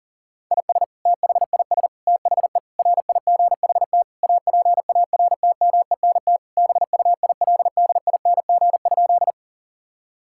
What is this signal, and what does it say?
Morse code, heard as a keyed tone, about 40 wpm, 700 Hz. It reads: IS THIS THE RIGHT APARTMENT BUILDING?
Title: Sentences from Top 1000 Words - Review Part 6 - 40wpm